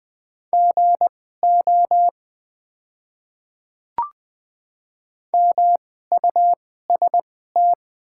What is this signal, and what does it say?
Morse code, heard as a keyed tone, about 20 wpm, spaced as for 14 wpm, 700 Hz. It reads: GO MUST
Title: 2x Character Spacing - Top 300 Words - 20wpm